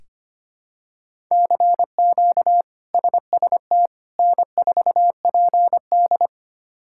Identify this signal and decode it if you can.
Morse code, heard as a keyed tone, about 25 wpm, 700 Hz. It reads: CQ SST N4PD
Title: K1USN SST QSO Practice - 25wpm